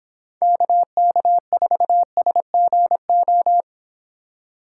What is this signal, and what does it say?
Morse code, heard as a keyed tone, about 30 wpm, 700 Hz. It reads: KK4SGO